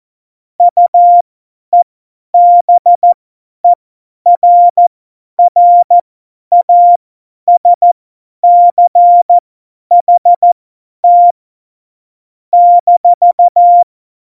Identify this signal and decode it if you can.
Morse code, heard as a keyed tone, about 14 wpm, spaced as for 10 wpm, 700 Hz. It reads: UEBERRASCHT -
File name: Morsekurs / Witze / Witz 27 Juni 17, Tempo 50 F70